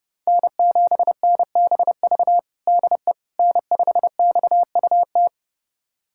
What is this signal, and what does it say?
Morse code, heard as a keyed tone, about 32 wpm, 700 Hz. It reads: N7NBV DE N5XUT
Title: USA Callsign DE Callsign - 5 Characters - 30wpm